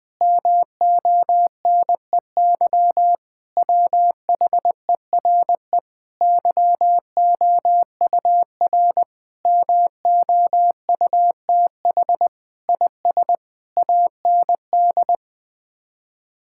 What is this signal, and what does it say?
Morse code, heard as a keyed tone, about 20 wpm, 700 Hz. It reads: MONEY WHERE YOUR MOUTH IS AND
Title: Sets of 8 Words - 20wpm